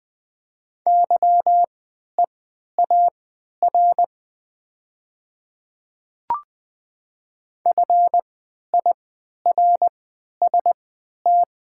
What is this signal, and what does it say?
Morse code, heard as a keyed tone, about 20 wpm, spaced as for 11 wpm, 700 Hz. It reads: YEAR FIRST
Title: 3x Character Spacing - Top 300 Words - 20wpm